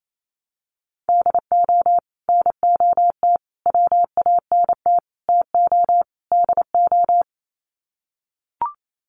Morse code, heard as a keyed tone, about 28 wpm, 700 Hz. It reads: DO NOT WANT TO DO
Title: Sets of 5 Words 28wpm